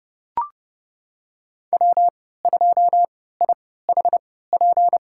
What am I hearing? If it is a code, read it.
Morse code, heard as a keyed tone, about 30 wpm, spaced as for 17 wpm, 700 Hz. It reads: W2IHP
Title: USA Call Signs - 5 Characters - 3x Character Spacing - 30wpm